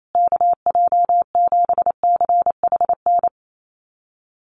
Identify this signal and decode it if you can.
Morse code, heard as a keyed tone, about 28 wpm, 700 Hz. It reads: KJ7CHN
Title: USA Call Signs Repeated 3x 28wpm